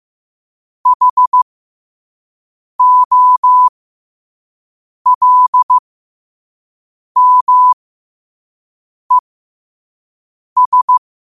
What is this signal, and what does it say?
Morse code, heard as a keyed tone, about 15 wpm, spaced as for 5 wpm, 1000 Hz. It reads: HOLMES
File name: qotdfiles / 5wpm